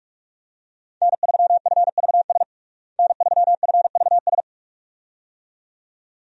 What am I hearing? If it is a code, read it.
Morse code, heard as a keyed tone, about 41 wpm, 700 Hz. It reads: N3FVS N3FVS
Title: USA Call Signs - 5 Characters - Repeated 2x - 45wpm